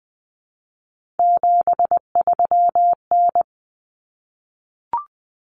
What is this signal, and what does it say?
Morse code, heard as a keyed tone, about 20 wpm, 700 Hz. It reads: 73N
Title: Common Cut Numbers - 3-Digit 20wpm